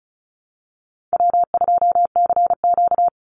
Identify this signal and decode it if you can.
Morse code, heard as a keyed tone, about 34 wpm, 700 Hz. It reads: W2CQ
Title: Easy Call Signs Repeated 2x - 35wpm